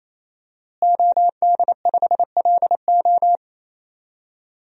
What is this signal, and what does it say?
Morse code, heard as a keyed tone, about 28 wpm, 700 Hz. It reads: OD5LO
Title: DXCC-100 Call Signs - 5 Characters - Repeated 3x - 28wpm